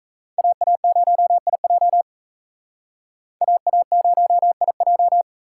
Morse code, heard as a keyed tone, about 35 wpm, 700 Hz. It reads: AA0IJ AA0IJ
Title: Speed Racing - USA 2x2 Call Signs 28wpm